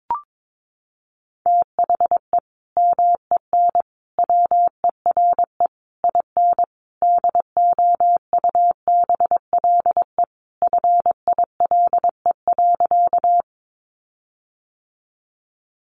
Morse code, heard as a keyed tone, about 22 wpm, 700 Hz. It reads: THE MEN WERE IN DOUBLE FILE.
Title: Sentences from Top 1400 Words 22wpm